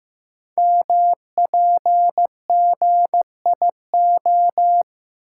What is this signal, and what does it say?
Morse code, heard as a keyed tone, about 15 wpm, 700 Hz. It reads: MPGIO
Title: Code Groups - 5 Letters - 15wpm